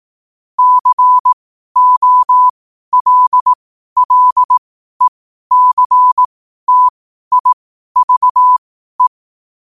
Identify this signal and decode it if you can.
Morse code, heard as a keyed tone, about 18 wpm, spaced as for 12 wpm, 1000 Hz. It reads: COLLECTIVE
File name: qotdfiles / 12wpm